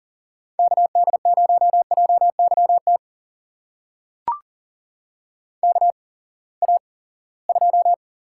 Morse code, heard as a keyed tone, about 40 wpm, spaced as for 23 wpm, 700 Hz. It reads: KD0JYT KA2
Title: USA Call Signs - 8x Character Spacing - 40wpm